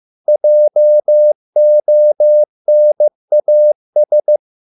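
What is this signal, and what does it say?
Morse code, heard as a keyed tone, about 15 wpm, 600 Hz. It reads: JONAS
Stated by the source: Morse code at a slow 15 wpm from CW med Gnister.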